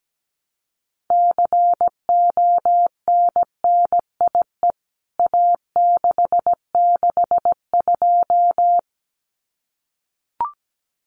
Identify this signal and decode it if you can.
Morse code, heard as a keyed tone, about 17 wpm, 700 Hz. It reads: CONNIE A662
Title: CWT Exchange 17wpm